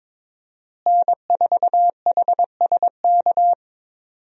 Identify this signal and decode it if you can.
Morse code, heard as a keyed tone, about 22 wpm, 700 Hz. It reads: N4HSK
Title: USA Call Signs - 5 Characters - Repeated 3x - 22wpm